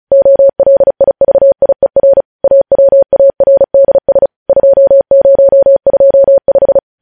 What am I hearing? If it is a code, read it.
Morse code, heard as a keyed tone, about 36 wpm, 550 Hz. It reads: OLIVIER AWARDS 2025
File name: podcast / News headlines-1x-35x35